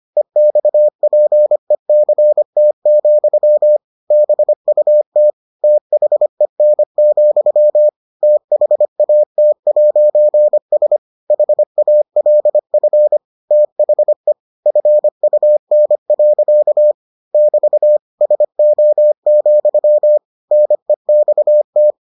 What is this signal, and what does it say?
Morse code, heard as a keyed tone, about 25 wpm, 600 Hz. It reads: EXPECT, BUT THEN, THAT'S HALF THE FUN. <BT> SO, NEXT